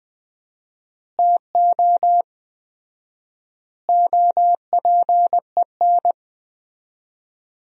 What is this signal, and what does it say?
Morse code, heard as a keyed tone, about 20 wpm, 700 Hz. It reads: TO OPEN
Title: 4x Word-Spacing - Sentences from Top 300 Words - 20wpm